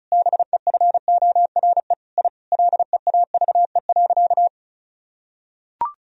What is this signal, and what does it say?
Morse code, heard as a keyed tone, about 36 wpm, 700 Hz. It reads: BEFORE I LEAVE.
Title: Sentences from Top 1000 Words - Review Part 4 - 35wpm